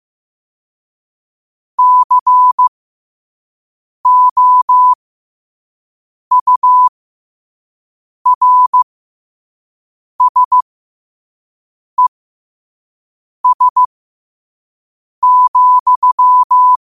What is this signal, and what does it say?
Morse code, heard as a keyed tone, about 15 wpm, spaced as for 5 wpm, 1000 Hz. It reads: COURSES,
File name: qotdfiles / 5wpm